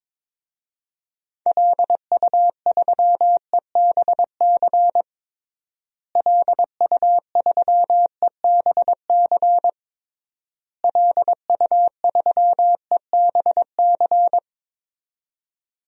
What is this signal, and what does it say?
Morse code, heard as a keyed tone, about 22 wpm, 700 Hz. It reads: LU3EBC LU3EBC LU3EBC
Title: DXCC-100 Call Signs - 6 Characters - Repeated 3x - 22wpm